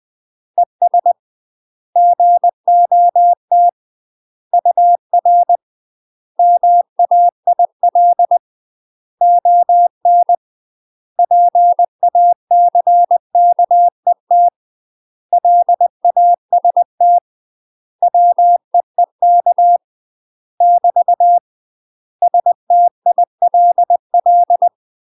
Morse code, heard as a keyed tone, about 20 wpm, 700 Hz. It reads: ES GOT UR MAIL ON PACKET LAST WEEK <BT> STILL
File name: UK MASTER QSOs TRAINING / 20WPM / UK MASTER QSOs 0009